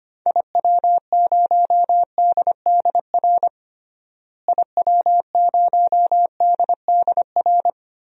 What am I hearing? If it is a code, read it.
Morse code, heard as a keyed tone, about 25 wpm, 700 Hz. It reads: IW0DDR IW0DDR
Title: DXCC-100 Call Signs - 6 Characters - Repeated 3x - 25wpm